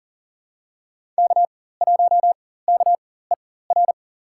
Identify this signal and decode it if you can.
Morse code, heard as a keyed tone, about 40 wpm, spaced as for 19 wpm, 700 Hz. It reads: K1KER
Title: USA Call Signs - 4x Character Spacing - 40wpm